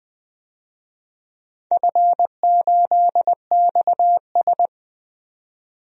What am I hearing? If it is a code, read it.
Morse code, heard as a keyed tone, about 20 wpm, 700 Hz. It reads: F8XS
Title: DXCC-100 Call Signs - 4 Characters - 20wpm